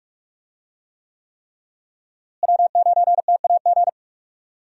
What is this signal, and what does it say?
Morse code, heard as a keyed tone, about 46 wpm, 700 Hz. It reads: W9TAG
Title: USA Call Signs - 5 Characters - Repeated 3x - 45wpm